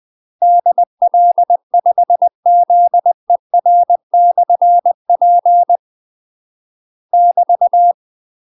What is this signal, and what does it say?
Morse code, heard as a keyed tone, about 20 wpm, 700 Hz. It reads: DL5ZER/P <BT>
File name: UK MASTER QSOs TRAINING / 20WPM / UK MASTER QSOs 0043